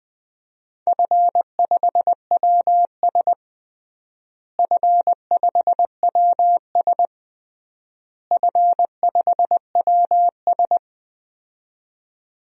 Morse code, heard as a keyed tone, about 20 wpm, 700 Hz. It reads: F5WS F5WS F5WS
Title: DXCC-100 Call Signs - 4 Characters - Repeated 3x - 20wpm